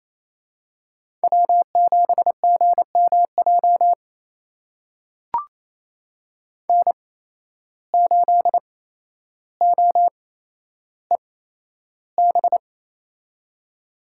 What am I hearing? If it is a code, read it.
Morse code, heard as a keyed tone, about 28 wpm, spaced as for 12 wpm, 700 Hz. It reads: W7GMJ N8OEB
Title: USA Call Signs - 5 Characters - 8x Character Spacing - 28wpm